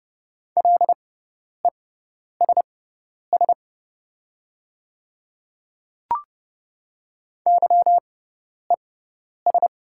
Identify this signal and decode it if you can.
Morse code, heard as a keyed tone, about 30 wpm, spaced as for 10 wpm, 690 Hz. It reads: LESS YES
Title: 6x Character Spacing - Top 500 Words - 30wpm